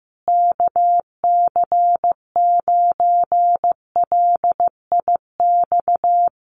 Morse code, heard as a keyed tone, about 15 wpm, 700 Hz. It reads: KC9LIX
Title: USA Call Signs Repeated 3x 15wpm